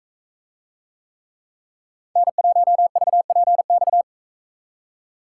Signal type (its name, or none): Morse code